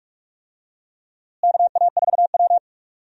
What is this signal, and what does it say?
Morse code, heard as a keyed tone, about 46 wpm, 700 Hz. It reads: KA4W
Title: USA Call Signs - 4 Characters - Repeated 3x - 45wpm